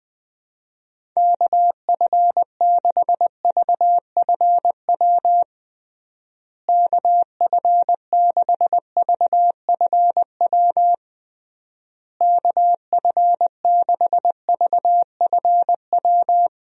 Morse code, heard as a keyed tone, about 20 wpm, 700 Hz. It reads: KF6VFW KF6VFW KF6VFW
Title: USA Call Signs - 6 Characters - Repeated 3x - 20wpm